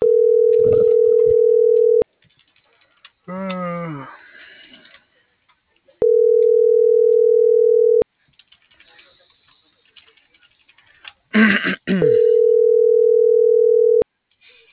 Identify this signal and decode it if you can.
Ringback tone (North America), 440 + 480 Hz.